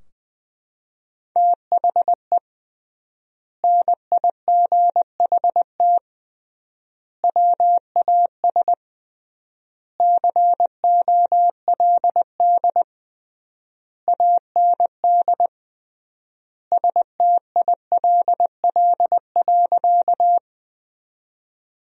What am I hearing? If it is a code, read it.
Morse code, heard as a keyed tone, about 20 wpm, 700 Hz. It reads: THE NIGHT WAS COLD AND STILL.
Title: 3x Word-Spacing - Sentences from Top 400 Words - 20wpm